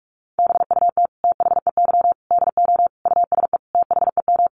Morse code, heard as a keyed tone, about 44 wpm, 700 Hz. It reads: BUT THEY DO USE THEM
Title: Sentences from Top 1300 Words 45wpm